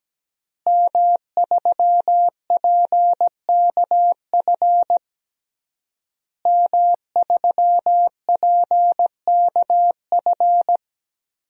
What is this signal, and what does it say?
Morse code, heard as a keyed tone, about 17 wpm, 700 Hz. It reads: M3PKF M3PKF
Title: DXCC-100 Call Signs - 5 Characters - Repeated 3x - 17wpm